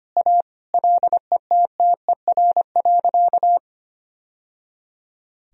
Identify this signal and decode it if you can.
Morse code, heard as a keyed tone, about 25 wpm, 700 Hz. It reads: A LETTER.
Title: Sentences from Top 1000 Words - Review Part 1 - 25wpm